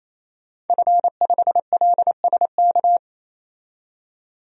Morse code, heard as a keyed tone, about 28 wpm, 700 Hz. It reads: F5LSK
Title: DXCC-100 Call Signs - 5 Characters - 28wpm